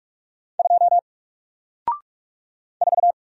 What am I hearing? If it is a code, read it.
Morse code, heard as a keyed tone, about 44 wpm, 700 Hz. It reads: 2 4
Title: Single Letter-Number Alt 45wpm